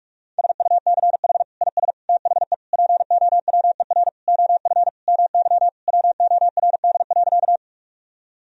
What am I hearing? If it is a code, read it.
Morse code, heard as a keyed tone, about 42 wpm, 700 Hz. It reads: SUCH IS THE POWER OF MY WORD.
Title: Sentences from Top 1000 Words - Review Part 2 - 45wpm